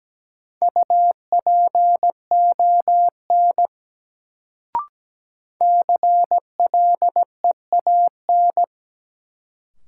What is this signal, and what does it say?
Morse code, heard as a keyed tone, about 17 wpm, 700 Hz. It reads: UPON CLEAN
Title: Sentences from Top 1100 Words - New Word Drill Alt - 17wpm